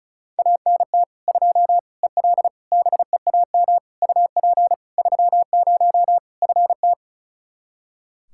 Morse code, heard as a keyed tone, about 34 wpm, 700 Hz. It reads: ANT 2 EL BEAM UP 30 FT
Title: Practice QSO - Exchange 2 - Info - 35wpm